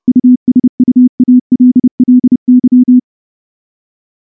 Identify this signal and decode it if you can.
Morse code, heard as a keyed tone, about 28 wpm, 250 Hz. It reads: USUALLY
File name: morse-training-30WPM